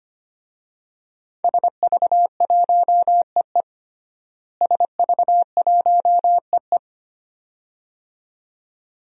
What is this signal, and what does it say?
Morse code, heard as a keyed tone, about 25 wpm, 700 Hz. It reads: SV1EE SV1EE
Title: DXCC-100 Call Signs - 5 Characters - Repeated 2x - 25wpm